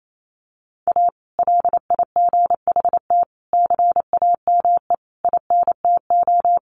Morse code, heard as a keyed tone, about 28 wpm, 700 Hz. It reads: A LIGHT CAME INTO
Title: Sentences from Top 1100 Words 28wpm